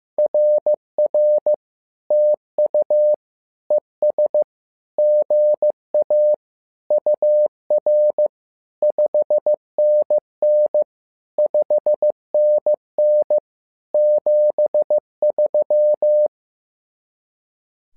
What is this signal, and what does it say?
Morse code, heard as a keyed tone, about 15 wpm, 600 Hz. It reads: RR TU ES GA UR 5NN 5NN 73
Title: SOTA QSO Practice 15wpm